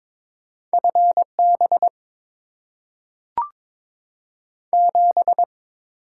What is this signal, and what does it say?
Morse code, heard as a keyed tone, about 22 wpm, 700 Hz. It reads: FB 7
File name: Top 20 QSO Elements - 8x Character Spacing - 22wpm